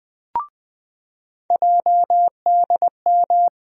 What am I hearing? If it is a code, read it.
Morse code, heard as a keyed tone, about 21 wpm, 700 Hz. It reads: JDM
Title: USA Call Sign Postfixes - 3 Letters 20wpm